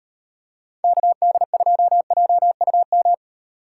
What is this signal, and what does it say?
Morse code, heard as a keyed tone, about 36 wpm, 700 Hz. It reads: KD2JUM